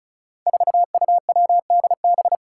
Morse code, heard as a keyed tone, about 36 wpm, 700 Hz. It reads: 4UWDB
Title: Five Challenging Letter-Number Combinations 35wpm